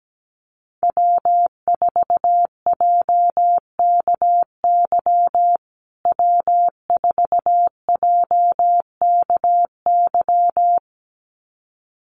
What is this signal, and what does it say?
Morse code, heard as a keyed tone, about 17 wpm, 700 Hz. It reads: W4JKY W4JKY
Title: USA Call Signs Repeated 2x 17wpm